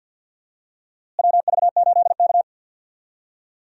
Morse code, heard as a keyed tone, about 50 wpm, 700 Hz. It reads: WV8K
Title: USA Call Signs - 4 Characters - Repeated 3x - 50wpm